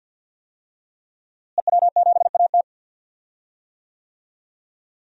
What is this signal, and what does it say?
Morse code, heard as a keyed tone, about 49 wpm, 700 Hz. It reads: EW7AT
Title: DXCC-100 Call Signs - 5 Characters - Repeated 3x - 50wpm